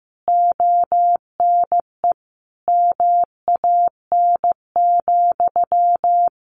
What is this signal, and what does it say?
Morse code, heard as a keyed tone, about 15 wpm, 700 Hz. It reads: ONE MAN,
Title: A Sound Of Thunder 15wpm